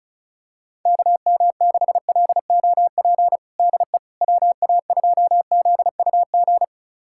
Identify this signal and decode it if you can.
Morse code, heard as a keyed tone, about 36 wpm, 700 Hz. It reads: KM6LOP DE WA2ZUG
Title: USA Callsign DE Callsign - 6 Characters - 35wpm